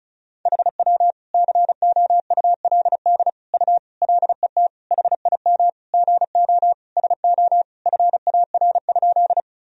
Morse code, heard as a keyed tone, about 36 wpm, 700 Hz. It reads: HW COULD U LET HIM GO SO FAR?
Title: Sentences from Top 300 Words and Abbreviations 35wpm